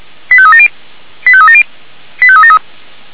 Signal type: five-tone selective call